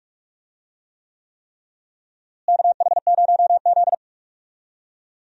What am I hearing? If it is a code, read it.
Morse code, heard as a keyed tone, about 46 wpm, 700 Hz. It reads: KH0Z
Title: USA Call Signs - 4 Characters - Repeated 3x - 45wpm